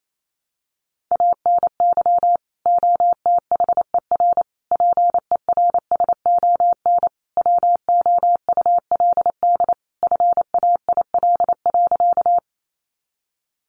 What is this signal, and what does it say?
Morse code, heard as a keyed tone, about 28 wpm, 700 Hz. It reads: ANY OTHER PERSON WOULD FAIL.
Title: Sentences from Top 1100 Words 28wpm